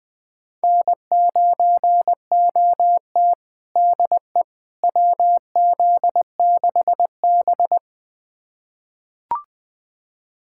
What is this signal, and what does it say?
Morse code, heard as a keyed tone, about 20 wpm, 700 Hz. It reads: N9OT DE WZ6B
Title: USA Callsign DE Callsign - 4 Characters - 20wpm